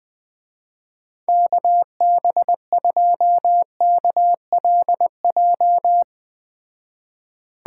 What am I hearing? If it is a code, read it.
Morse code, heard as a keyed tone, about 20 wpm, 700 Hz. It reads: KB2KLJ
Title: Speed Racing - USA 2x3 Call Signs 17wpm